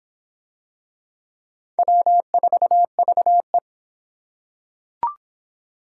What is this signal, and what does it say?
Morse code, heard as a keyed tone, about 25 wpm, 700 Hz. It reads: W4VE